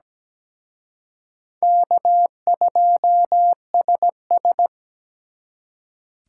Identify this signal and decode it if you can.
Morse code, heard as a keyed tone, about 20 wpm, 700 Hz. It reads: K2SS
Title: Speed Racing - USA 1x2 Call Signs 17wpm